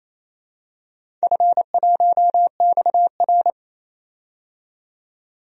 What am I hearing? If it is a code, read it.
Morse code, heard as a keyed tone, about 28 wpm, 700 Hz. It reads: F1XR
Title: DXCC-100 Call Signs - 4 Characters - 28wpm